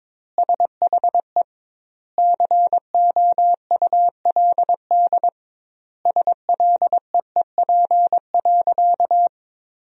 Morse code, heard as a keyed tone, about 22 wpm, 700 Hz. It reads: SHE COULD SLEEP.
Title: 2x Word-Spacing - Sentences from Top 500 Words - 22wpm